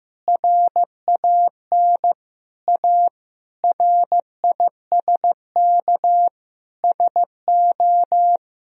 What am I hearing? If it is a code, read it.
Morse code, heard as a keyed tone, about 15 wpm, 700 Hz. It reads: RAN A RISK SO
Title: Sentences from Top 1000 Words - Review Part 4 - 15wpm